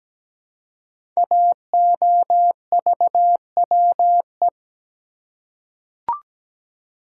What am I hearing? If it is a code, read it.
Morse code, heard as a keyed tone, about 17 wpm, 700 Hz. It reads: AOVWE
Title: Code Groups - 5 Letters - 17wpm